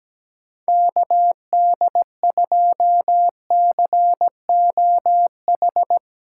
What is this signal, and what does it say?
Morse code, heard as a keyed tone, about 17 wpm, 700 Hz. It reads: KD2COH
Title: USA Call Signs - 3x Character Spacing - 17wpm